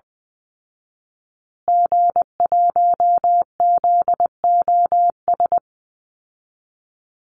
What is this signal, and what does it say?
Morse code, heard as a keyed tone, about 20 wpm, 700 Hz. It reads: G1ZOS
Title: Call Signs Repeated 2x - 20wpm